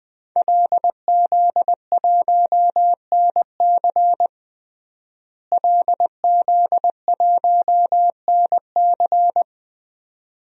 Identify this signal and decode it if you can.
Morse code, heard as a keyed tone, about 20 wpm, 700 Hz. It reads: LZ1NC LZ1NC
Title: DXCC-100 Call Signs - 5 Characters - Repeated 2x - 20wpm